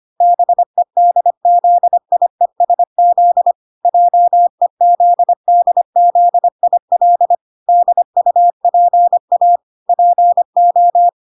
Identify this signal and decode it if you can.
Morse code, heard as a keyed tone, about 25 wpm, 700 Hz. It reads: BEDZIESZ JEZDZIL DUPA PO